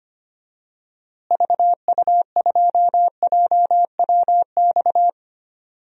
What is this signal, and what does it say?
Morse code, heard as a keyed tone, about 25 wpm, 700 Hz. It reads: VU2JWX
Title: DXCC-100 Call Signs - 6 Characters - Repeated 2x - 25wpm